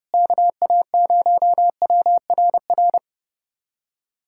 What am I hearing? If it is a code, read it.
Morse code, heard as a keyed tone, about 30 wpm, 700 Hz. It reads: KA0WRR
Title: USA Callsign DE Callsign 30wpm